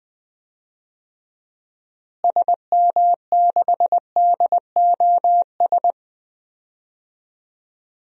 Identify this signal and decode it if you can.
Morse code, heard as a keyed tone, about 20 wpm, 700 Hz. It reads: SM6DOS
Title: DXCC-100 Call Signs - 6 Characters - Repeated 2x - 20wpm